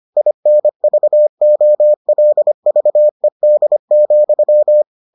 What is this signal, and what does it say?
Morse code, heard as a keyed tone, about 25 wpm, 600 Hz. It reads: INVOLVED,